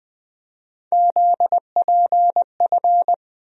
Morse code, heard as a keyed tone, about 20 wpm, 700 Hz. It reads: ZPF
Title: USA Call Sign Postfixes - 3 Letters 20wpm